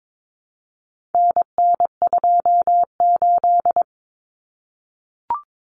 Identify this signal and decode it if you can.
Morse code, heard as a keyed tone, about 22 wpm, 700 Hz. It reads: NN28
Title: Common Cut Numbers - 4-Digit 22wpm